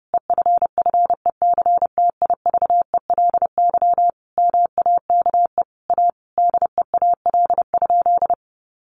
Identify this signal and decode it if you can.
Morse code, heard as a keyed tone, about 30 wpm, 700 Hz. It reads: EFFECTIVELY MAKE A DEAL?
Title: Sentences from Top 1400 Words 30wpm